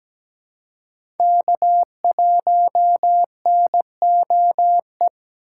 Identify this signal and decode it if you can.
Morse code, heard as a keyed tone, about 17 wpm, 700 Hz. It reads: K1NOE
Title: USA Call Signs - 6x Character Spacing - 17wpm